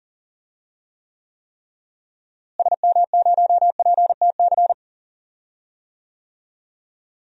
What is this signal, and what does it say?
Morse code, heard as a keyed tone, about 40 wpm, 700 Hz. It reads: SM0PTC
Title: DXCC-100 Call Signs - 6 Characters - Repeated 3x - 40wpm